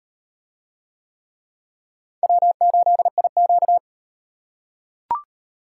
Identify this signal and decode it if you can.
Morse code, heard as a keyed tone, about 32 wpm, 700 Hz. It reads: W8IQ